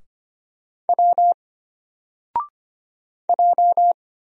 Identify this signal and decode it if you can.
Morse code, heard as a keyed tone, about 25 wpm, 700 Hz. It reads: W J
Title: Single Letter-Number Alt 25wpm